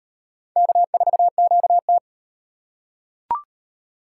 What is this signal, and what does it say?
Morse code, heard as a keyed tone, about 35 wpm, 700 Hz. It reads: K4QT